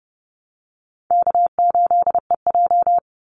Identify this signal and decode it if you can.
Morse code, heard as a keyed tone, about 30 wpm, 700 Hz. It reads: K8EJ
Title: Easy Call Signs Repeated 2x - 30wpm